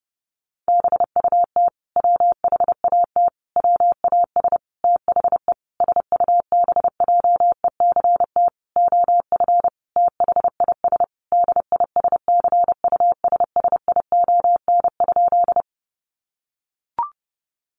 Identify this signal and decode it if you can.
Morse code, heard as a keyed tone, about 30 wpm, 700 Hz. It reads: BUT WHAT WAS THE SUBJECT OF THIS DISCUSSION?
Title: Sentences from Top 1100 Words 30wpm